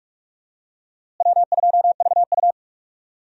Morse code, heard as a keyed tone, about 45 wpm, 700 Hz. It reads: W2VU
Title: USA Call Signs - 4 Characters - Repeated 2x - 45wpm